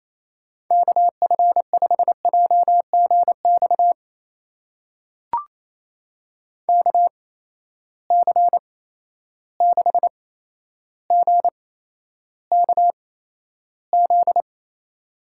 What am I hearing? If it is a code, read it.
Morse code, heard as a keyed tone, about 28 wpm, spaced as for 12 wpm, 700 Hz. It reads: KF5JGX KC6GKZ
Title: USA Call Signs - 6 Characters - 8x Character Spacing - 28wpm